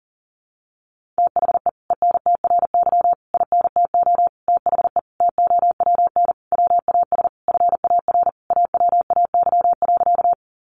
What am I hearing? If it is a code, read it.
Morse code, heard as a keyed tone, about 37 wpm, 700 Hz. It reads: THE ENTRY INTO THE TOWN WAS FAR AWAY.
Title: Sentences from Top 1500 Words 40wpm